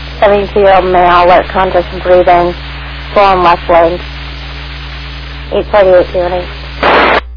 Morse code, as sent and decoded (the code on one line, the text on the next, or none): none